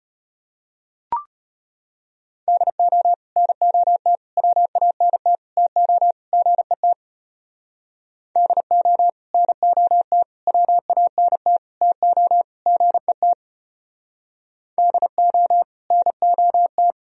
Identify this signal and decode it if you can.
Morse code, heard as a keyed tone, about 31 wpm, 700 Hz. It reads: DO NOT WANT TO GET DO NOT WANT TO GET DO NOT
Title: Speed-Racing - Sets of 5 Words 25wpm